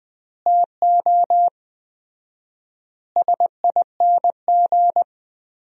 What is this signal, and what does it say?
Morse code, heard as a keyed tone, about 20 wpm, 700 Hz. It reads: TO SING
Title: 4x Word-Spacing - Sentences from Top 400 Words - 20wpm